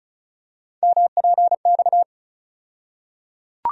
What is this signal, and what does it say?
Morse code, heard as a keyed tone, about 36 wpm, 700 Hz. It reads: MPX
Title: USA Call Sign Postfixes - 3 Letters 35wpm